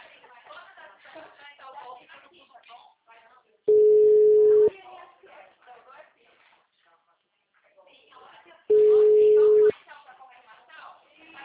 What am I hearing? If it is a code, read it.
Ringback tone (Europe, 425 Hz), with speech.